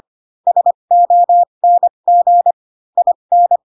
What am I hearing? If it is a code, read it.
Morse code, heard as a keyed tone, about 25 wpm, spaced as for 22 wpm, 700 Hz. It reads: SONG IN